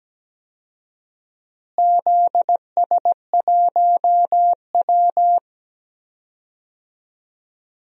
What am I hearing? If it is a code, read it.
Morse code, heard as a keyed tone, about 17 wpm, 700 Hz. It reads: ZS1W
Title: DXCC-100 Call Signs - 4 Characters - Repeated 2x - 17wpm